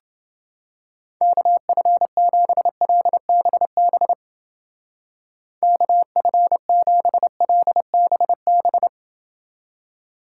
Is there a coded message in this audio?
Yes: Morse code from 1.2 to 8.9 s, heard as a keyed tone, about 29 wpm, 700 Hz. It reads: KF7LBB KF7LBB